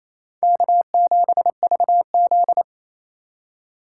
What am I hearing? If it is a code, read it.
Morse code, heard as a keyed tone, about 36 wpm, 700 Hz. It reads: K7VZ